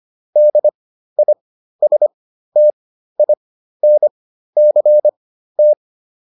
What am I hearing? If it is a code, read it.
Morse code, heard as a keyed tone, about 25 wpm, spaced as for 13 wpm, 600 Hz. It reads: DISTINCT